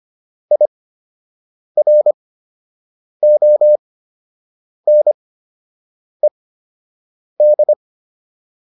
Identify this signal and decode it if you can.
Morse code, heard as a keyed tone, about 25 wpm, spaced as for 7 wpm, 600 Hz. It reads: IRONED
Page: fundamental-practice-files